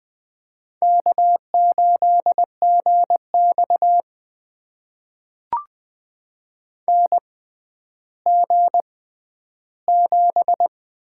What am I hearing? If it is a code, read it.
Morse code, heard as a keyed tone, about 21 wpm, spaced as for 11 wpm, 700 Hz. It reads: K8GX NG7
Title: USA Call Signs - 4 Characters - 6x Character Spacing - 20wpm